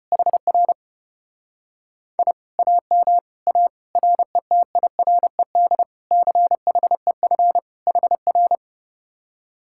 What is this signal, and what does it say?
Morse code, heard as a keyed tone, about 33 wpm, 700 Hz. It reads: HR I AM A RETIRED CHEF HR